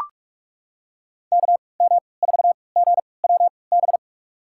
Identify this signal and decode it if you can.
Morse code, heard as a keyed tone, about 44 wpm, spaced as for 25 wpm, 700 Hz. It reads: KM4GWB
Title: USA Call Signs - 6 Characters - 3x Character Spacing - 45wpm